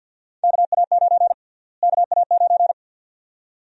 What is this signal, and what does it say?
Morse code, heard as a keyed tone, about 49 wpm, 700 Hz. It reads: KA9 KA9
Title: USA Call Sign Prefixes - Repeated 2x - 50wpm